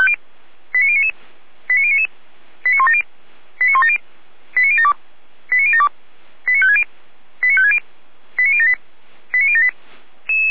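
Five-tone selective call (ZVEI), heard as quick runs of five tones, tones of about 70 ms. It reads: 89099 89099 89170 89170 89072 89072 89560 89560 89078 89078
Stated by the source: Received by SDR.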